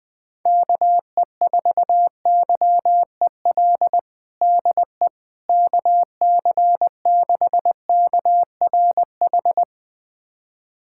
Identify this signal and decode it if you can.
Morse code, heard as a keyed tone, about 20 wpm, 700 Hz. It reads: KE4YEL DE KC6KRH
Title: USA Callsign DE Callsign 20wpm